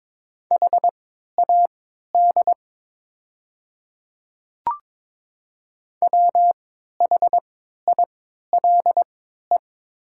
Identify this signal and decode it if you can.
Morse code, heard as a keyed tone, about 22 wpm, spaced as for 12 wpm, 700 Hz. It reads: HAD WHILE
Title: 3x Character Spacing - Top 300 Words - 22wpm